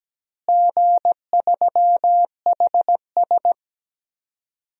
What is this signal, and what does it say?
Morse code, heard as a keyed tone, about 17 wpm, 700 Hz. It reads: G3HS
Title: DXCC-100 Call Signs - 4 Characters - Repeated 3x - 17wpm